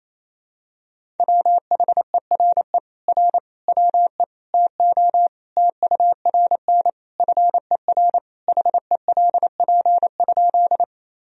Morse code, heard as a keyed tone, about 28 wpm, 700 Hz. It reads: WHERE R WE TO TURN FER HELP?
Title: Sentences from Top 200 Words and Abbreviations 28wpm